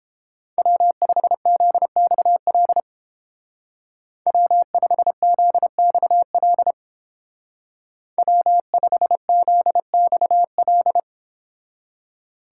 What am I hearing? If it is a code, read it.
Morse code, heard as a keyed tone, about 29 wpm, 700 Hz. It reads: W5ZXL W5ZXL W5ZXL